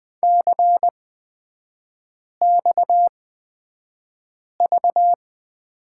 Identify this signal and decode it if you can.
Morse code, heard as a keyed tone, about 20 wpm, spaced as for 5 wpm, 700 Hz. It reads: CXV
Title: 243 Lesson 39e - Letter z - Three Characters 20wpm